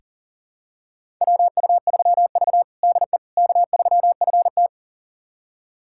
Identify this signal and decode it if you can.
Morse code, heard as a keyed tone, about 40 wpm, 700 Hz. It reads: WU3V DE K3FT
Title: USA Callsign DE Callsign - 4 Characters - 40wpm